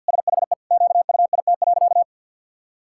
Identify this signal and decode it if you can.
Morse code, heard as a keyed tone, about 52 wpm, 700 Hz. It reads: SHE QUIT.
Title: Pangram Challenge v2 - 50wpm